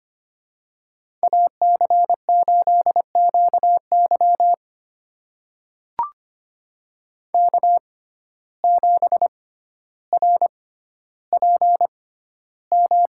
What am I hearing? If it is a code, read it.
Morse code, heard as a keyed tone, about 25 wpm, spaced as for 13 wpm, 700 Hz. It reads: AC8QY K7RPM
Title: USA Call Signs - 5 Characters - 6x Character Spacing - 25wpm